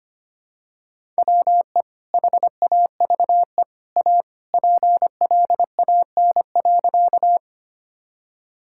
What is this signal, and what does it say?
Morse code, heard as a keyed tone, about 25 wpm, 700 Hz. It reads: WE HAVE A PLAN.
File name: practice / 25wpm / Sentences with 2-Word Phrases - 25wpm